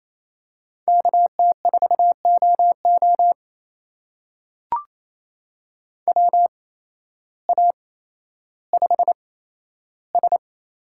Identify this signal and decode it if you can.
Morse code, heard as a keyed tone, about 28 wpm, spaced as for 13 wpm, 700 Hz. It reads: KT4OO WA5S
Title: USA Call Signs - 8x Character Spacing - 28wpm